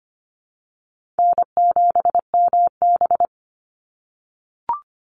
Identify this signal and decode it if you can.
Morse code, heard as a keyed tone, about 25 wpm, 700 Hz. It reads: N7MB
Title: Call Signs - 25wpm